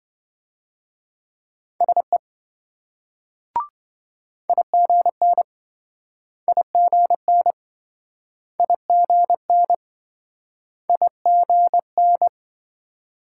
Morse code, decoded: SE IGN IGN IGN IGN

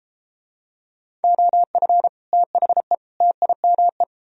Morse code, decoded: OF THE TIME

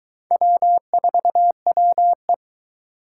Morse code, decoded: W4WE

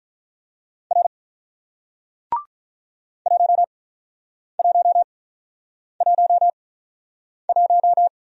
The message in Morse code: R 1111

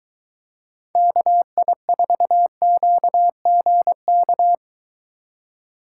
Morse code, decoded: KI4QGK

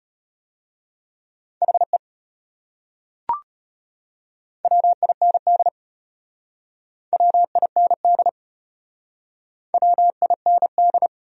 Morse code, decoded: HE WIND WIND WIND

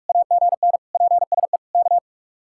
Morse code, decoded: AGN PSE K

